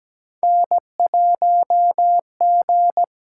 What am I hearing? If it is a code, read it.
Morse code, heard as a keyed tone, about 17 wpm, 700 Hz. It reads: N1G
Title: USA Call Signs - 8x Character Spacing - 17wpm